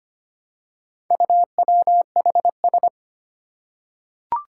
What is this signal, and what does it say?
Morse code, heard as a keyed tone, about 25 wpm, 700 Hz. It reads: UWHS
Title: Four Challenging Letter-Number Combinations 25wpm